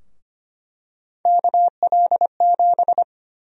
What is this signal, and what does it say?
Morse code, heard as a keyed tone, about 25 wpm, 700 Hz. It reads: KL7
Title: USA Call Sign Prefixes - 25wpm